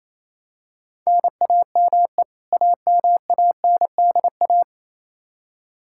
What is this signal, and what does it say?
Morse code, heard as a keyed tone, about 28 wpm, 700 Hz. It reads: NAME AMANDA
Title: Practice QSO - Exchange 1 - Beginning - 28wpm